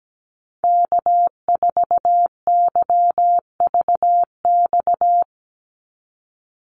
Morse code, heard as a keyed tone, about 17 wpm, 700 Hz. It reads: K4YVX
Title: USA Call Signs 17wpm